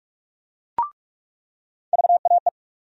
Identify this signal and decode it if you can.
Morse code, heard as a keyed tone, about 47 wpm, 700 Hz. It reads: VAE
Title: USA Call Sign Postfixes - 3 Letters 45wpm